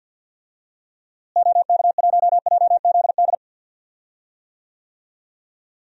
Morse code, heard as a keyed tone, about 52 wpm, 700 Hz. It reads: OK1JZD
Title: DXCC-100 Call Signs - 6 Characters - Repeated 2x - 50wpm